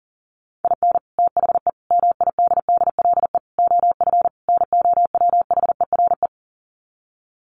Morse code, heard as a keyed tone, about 40 wpm, 700 Hz. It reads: IN THE MIDDLE OF NOWHERE
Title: Sets of 5 Words Encore 40wpm